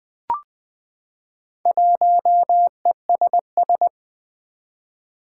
Morse code, heard as a keyed tone, about 20 wpm, 700 Hz. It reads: 1ESS
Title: Four Challenging Letter-Number Combinations 20wpm